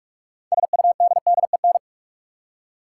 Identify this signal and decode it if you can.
Morse code, heard as a keyed tone, about 47 wpm, 700 Hz. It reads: SUDDEN
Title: Six Letter Words - 45wpm